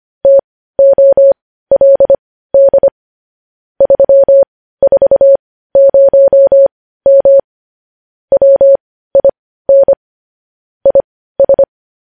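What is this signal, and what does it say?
Morse code, heard as a keyed tone, about 25 wpm, spaced as for 15 wpm, 550 Hz. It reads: TOLD 340M WIN IS
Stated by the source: Morse news headlines sent at 15 WPM, beginning with vvv.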